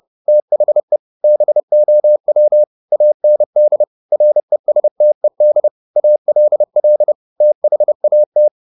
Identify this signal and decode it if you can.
Morse code, heard as a keyed tone, about 30 wpm, 600 Hz. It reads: THE BOW AND RESTED ALL THAT